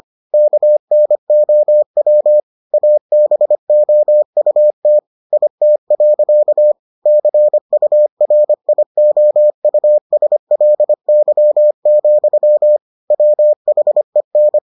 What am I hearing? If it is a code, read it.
Morse code, heard as a keyed tone, about 25 wpm, 600 Hz. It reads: KNOW ABOUT IT. CURIOUSLY, WHEN